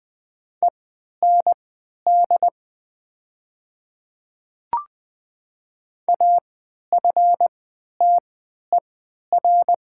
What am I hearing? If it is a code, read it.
Morse code, heard as a keyed tone, about 20 wpm, spaced as for 11 wpm, 700 Hz. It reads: END AFTER
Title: 3x Character Spacing - Top 300 Words - 20wpm